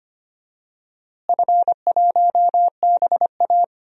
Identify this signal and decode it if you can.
Morse code, heard as a keyed tone, about 25 wpm, 700 Hz. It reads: F1BA
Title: DXCC-100 Call Signs - 4 Characters - Repeated 3x - 25wpm